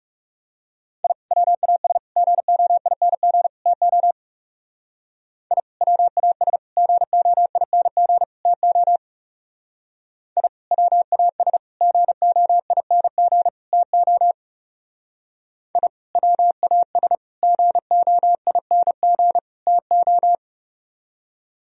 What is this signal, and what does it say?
Morse code, heard as a keyed tone, about 35 wpm, 700 Hz. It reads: I WAS GOING TO I WAS GOING TO I WAS GOING TO I WAS GOING TO
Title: Speed-Racing - Sets of 4 Words 30wpm